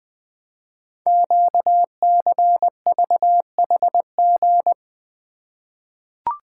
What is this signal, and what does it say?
Morse code, heard as a keyed tone, about 20 wpm, 700 Hz. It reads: QCVHG
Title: Code Groups - 5 Letters - 20wpm